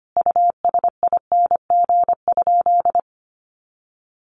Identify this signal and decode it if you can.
Morse code, heard as a keyed tone, about 25 wpm, 700 Hz. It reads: USING?
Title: Common QSO Phrases 25wpm